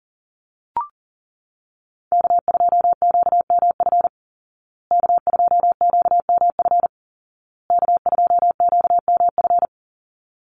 Morse code, heard as a keyed tone, about 40 wpm, 700 Hz. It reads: K2QMF K2QMF K2QMF
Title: Call Signs Repeated 2x - 40wpm